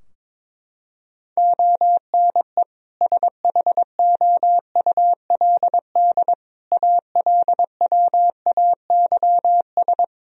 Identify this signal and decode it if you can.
Morse code, heard as a keyed tone, about 21 wpm, 700 Hz. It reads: ONE SHOULD ALWAYS
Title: Sentences from Top 1000 Words - Review Part 4 - 22wpm